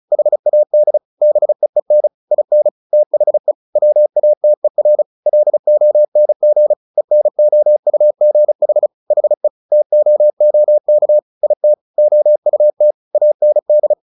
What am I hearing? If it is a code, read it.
Morse code, heard as a keyed tone, about 34 wpm, 600 Hz. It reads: HAD BEEN IN THE WATER LONG ENOUGH HE TOOK IT OUT AND
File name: the old man and the sea 2 35wpm